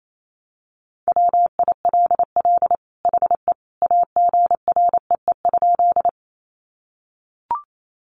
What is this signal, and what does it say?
Morse code, heard as a keyed tone, about 28 wpm, 700 Hz. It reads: WILL HE AGREE?
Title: Sentences from Top 1100 Words 28wpm